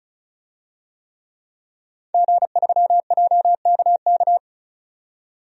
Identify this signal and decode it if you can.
Morse code, heard as a keyed tone, about 35 wpm, 700 Hz. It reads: G3JKK
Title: DXCC-100 Call Signs - 5 Characters - 35wpm